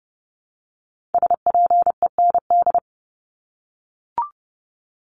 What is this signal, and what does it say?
Morse code, heard as a keyed tone, about 30 wpm, 700 Hz. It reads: SPEND
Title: Top 501-750 Words 30wpm